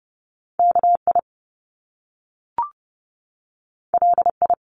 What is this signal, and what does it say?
Morse code, heard as a keyed tone, about 30 wpm, 700 Hz. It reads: K I L I
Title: Top Two Letter Combinations in Words 30wpm